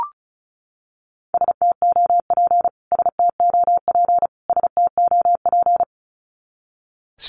Morse code, heard as a keyed tone, about 34 wpm, 700 Hz. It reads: STOP STOP STOP
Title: Top 300 Words Repeated 35wpm